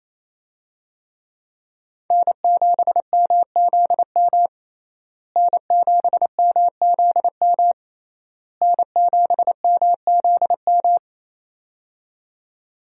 Morse code, heard as a keyed tone, about 27 wpm, 700 Hz. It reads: N7MZM N7MZM N7MZM
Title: USA Call Signs - 5 Characters - Repeated 3x - 28wpm